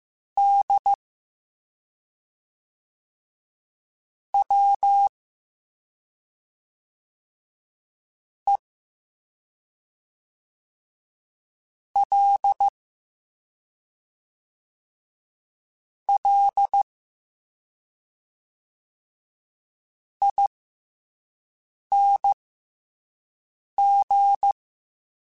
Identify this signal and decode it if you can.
Morse code, heard as a keyed tone, about 15 wpm, spaced as for 3 wpm, 800 Hz. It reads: DWELLING